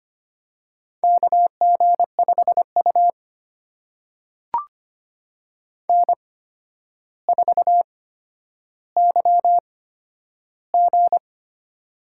Morse code, heard as a keyed tone, about 25 wpm, spaced as for 11 wpm, 700 Hz. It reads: KG5U N4YG
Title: USA Call Signs - 4 Characters - 8x Character Spacing - 25wpm